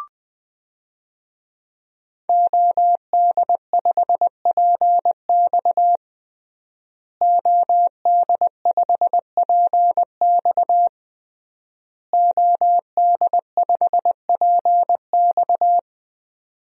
Morse code, heard as a keyed tone, about 20 wpm, 700 Hz. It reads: OD5PX OD5PX OD5PX
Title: DXCC-100 Call Signs - 5 Characters - Repeated 3x - 20wpm